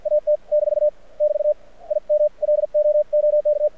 Morse code, heard as a keyed tone, about 53 wpm, 600 Hz. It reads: AT - <BT> I M LOOK
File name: archived / CRSnet